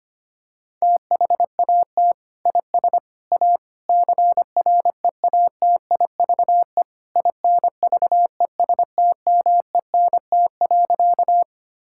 Morse code, heard as a keyed tone, about 25 wpm, 700 Hz. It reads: THAT IS A CREATIVE INVESTMENT.
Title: Sentences from Top 1000 Words - Review Part 5 - 25wpm